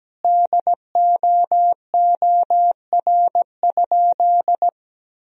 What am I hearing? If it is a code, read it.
Morse code, heard as a keyed tone, about 17 wpm, 700 Hz. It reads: DOOR?
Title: Sentences from Top 1000 Words - Review Part 4 - 17wpm